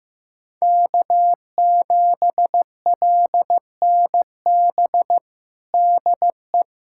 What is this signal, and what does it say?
Morse code, heard as a keyed tone, about 15 wpm, 700 Hz. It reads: K7LNB DE
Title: USA Callsign DE Callsign 15wpm